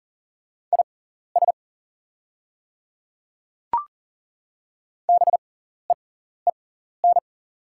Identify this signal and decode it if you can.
Morse code, heard as a keyed tone, about 40 wpm, spaced as for 14 wpm, 690 Hz. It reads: IS BEEN
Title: 6x Character Spacing - Top 300 Words - 40wpm